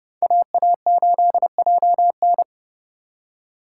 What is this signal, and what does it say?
Morse code, heard as a keyed tone, about 32 wpm, 700 Hz. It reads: AA8JN
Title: USA Call Signs - 2x Character Spacing - 30wpm